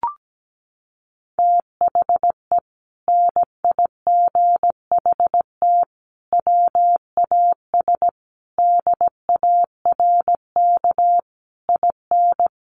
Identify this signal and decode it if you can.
Morse code, heard as a keyed tone, about 17 wpm, 700 Hz. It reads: THE NIGHT WAS DARK IN
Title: Sentences from Top 1300 Words 17wpm